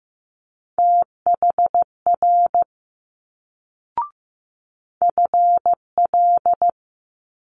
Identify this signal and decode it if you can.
Morse code, heard as a keyed tone, about 15 wpm, 700 Hz. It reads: THR FL
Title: Multi-letter Phonemes Quick Fire 15wpm